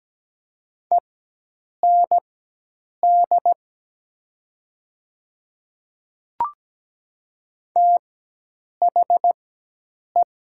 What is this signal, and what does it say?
Morse code, heard as a keyed tone, about 17 wpm, spaced as for 8 wpm, 700 Hz. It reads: END THE
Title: 4x Character Spacing - Top 300 Words - 17wpm